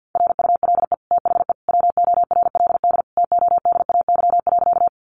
Morse code, heard as a keyed tone, about 50 wpm, 700 Hz. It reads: RULE THE WORLD TODAY.
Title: Sentences from Top 600 Words 50wpm